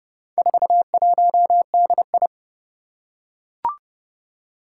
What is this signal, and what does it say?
Morse code, heard as a keyed tone, about 30 wpm, 700 Hz. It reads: 41DI
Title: Four Challenging Letter-Number Combinations 30wpm